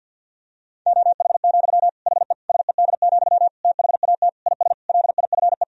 Morse code, heard as a keyed tone, about 52 wpm, 700 Hz. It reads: OH, HE SED, THAT IS LIFE
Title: Sentences from Top 500 Words and Abbreviations 50wpm